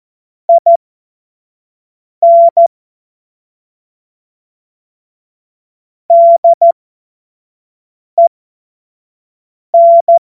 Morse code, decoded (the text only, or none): IN DEN